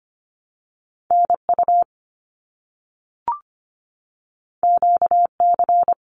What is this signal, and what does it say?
Morse code, heard as a keyed tone, about 25 wpm, 700 Hz. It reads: NU QC
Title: Canadian Provinces and Territories 25wpm